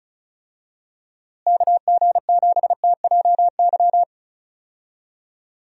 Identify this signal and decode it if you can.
Morse code, heard as a keyed tone, about 34 wpm, 700 Hz. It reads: KG7TJY
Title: USA Call Signs - 6 Characters - Repeated 3x - 35wpm